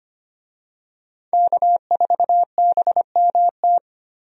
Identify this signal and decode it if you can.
Morse code, heard as a keyed tone, about 25 wpm, 700 Hz. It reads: K4BMT
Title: USA Call Signs - 5 Characters - Repeated 3x - 25wpm